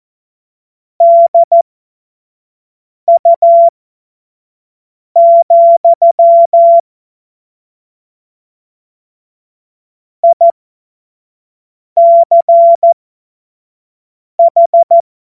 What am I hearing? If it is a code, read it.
Morse code, heard as a keyed tone, about 14 wpm, spaced as for 5 wpm, 680 Hz. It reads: DU, ICH
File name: Morsekurs / Witze / Witz 13. April 2021, 25 F70